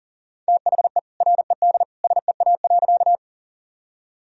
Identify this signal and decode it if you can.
Morse code, heard as a keyed tone, about 37 wpm, 700 Hz. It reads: THE RED SEA.
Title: Sentences from Top 1000 Words - Review Part 2 - 40wpm